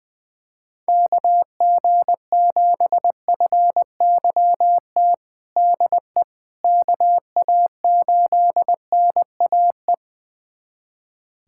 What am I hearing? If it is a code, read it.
Morse code, heard as a keyed tone, about 20 wpm, 700 Hz. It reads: KG7FYT DE KA8NAE
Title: USA Callsign DE Callsign - 6 Characters - 20wpm